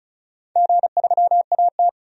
Morse code, heard as a keyed tone, about 36 wpm, 700 Hz. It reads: G3AT
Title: DXCC-100 Call Signs - 4 Characters - 35wpm